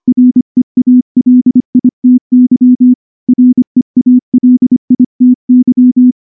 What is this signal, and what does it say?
Morse code, heard as a keyed tone, about 23 wpm, 250 Hz. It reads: REALITY REALITY
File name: morse / morse-training-25WPM